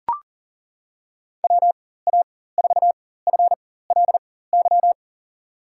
Morse code, heard as a keyed tone, about 40 wpm, spaced as for 19 wpm, 700 Hz. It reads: WA4FLY
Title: USA Call Signs - 6 Characters - 4x Character Spacing - 40wpm